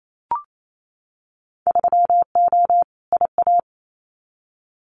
Morse code, heard as a keyed tone, about 28 wpm, 700 Hz. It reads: 3O IA
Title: Winter Field Day Exchanges 28wpm